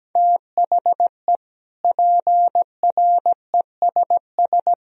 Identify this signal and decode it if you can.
Morse code, heard as a keyed tone, about 17 wpm, 700 Hz. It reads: THE PRESS
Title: Sentences from Top 1000 Words - Review Part 1 - 17wpm